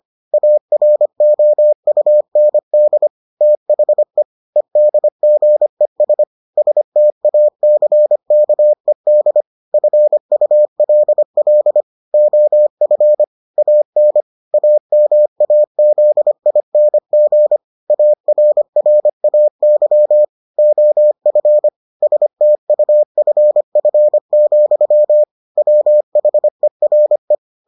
Morse code, heard as a keyed tone, about 25 wpm, 600 Hz. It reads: AROUND THE EDGES STACKED FULL OF AN AMAZING ARRAY OF STUFF, WHERE